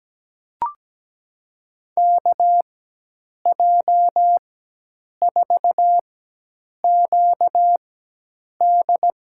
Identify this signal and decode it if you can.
Morse code, heard as a keyed tone, about 17 wpm, spaced as for 8 wpm, 700 Hz. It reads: KJ4QD
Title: USA Call Signs - 4x Character Spacing - 17wpm